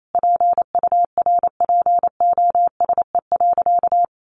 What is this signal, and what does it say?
Morse code, heard as a keyed tone, about 28 wpm, 700 Hz. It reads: PURPOSE.
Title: Sentences from Top 1500 Words 28wpm